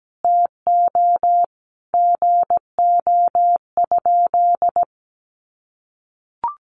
Morse code, heard as a keyed tone, about 17 wpm, 700 Hz. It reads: TO GO?
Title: Sentences from Top 1300 Words 17wpm